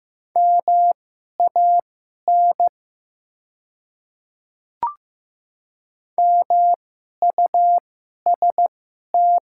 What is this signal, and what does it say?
Morse code, heard as a keyed tone, about 15 wpm, spaced as for 11 wpm, 700 Hz. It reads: MAN MUST
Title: 2x Character Spacing - Top 200 Words - 15wpm